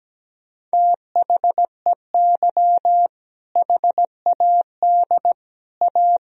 Morse code, heard as a keyed tone, about 17 wpm, 700 Hz. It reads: THEY HAD A
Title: Sentences from Top 1000 Words - Review Part 2 - 17wpm